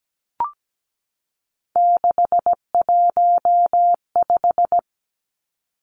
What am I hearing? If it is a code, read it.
Morse code, heard as a keyed tone, about 17 wpm, 700 Hz. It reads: 615
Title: Three Challenging Letter-Number Combinations 17wpm